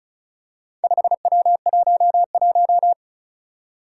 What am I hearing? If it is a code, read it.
Morse code, heard as a keyed tone, about 36 wpm, 700 Hz. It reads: 5W11
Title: Four Challenging Letter-Number Combinations 35wpm